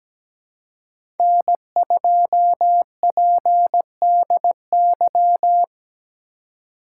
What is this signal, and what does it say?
Morse code, heard as a keyed tone, about 17 wpm, 700 Hz. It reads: N2PDY
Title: USA Call Signs - 3x Character Spacing - 17wpm